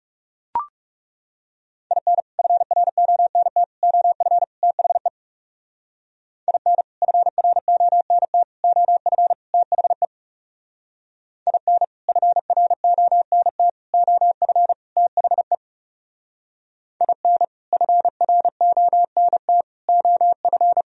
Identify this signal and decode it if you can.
Morse code, heard as a keyed tone, about 37 wpm, 700 Hz. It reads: IN FRONT OF THE IN FRONT OF THE IN FRONT OF THE IN FRONT OF
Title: Speed-Racing - Sets of 4 Words 30wpm